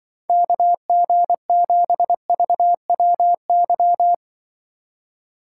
Morse code, heard as a keyed tone, about 24 wpm, 700 Hz. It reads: KG7VWY
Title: Speed Racing - USA 2x3 Call Signs 20wpm